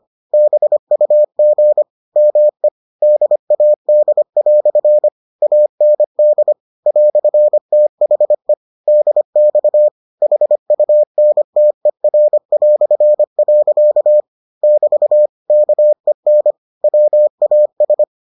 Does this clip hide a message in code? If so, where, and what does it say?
Yes: Morse code from 0.3 to 18.0 s, heard as a keyed tone, about 25 wpm, 600 Hz. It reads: BUG ME DAD" AND "THE DX HUNTER". <BT> KEN WAS